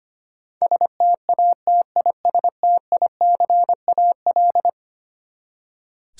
Morse code, heard as a keyed tone, about 25 wpm, 700 Hz. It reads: STATISTICAL
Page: morse-code-ninja-practice-25wpm